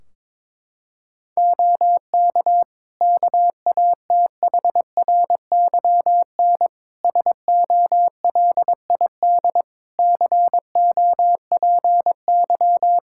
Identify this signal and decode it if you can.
Morse code, heard as a keyed tone, about 22 wpm, 700 Hz. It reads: OK KATHRYN SOLID COPY
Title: K1USN SST QSO Practice 22wpm